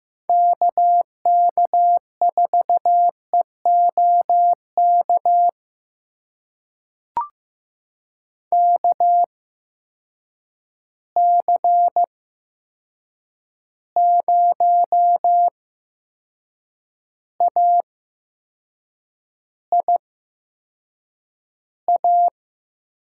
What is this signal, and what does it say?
Morse code, heard as a keyed tone, about 15 wpm, spaced as for 6 wpm, 700 Hz. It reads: KK4EOK KC0AIA
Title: USA Call Signs - 6 Characters - 8x Character Spacing - 15wpm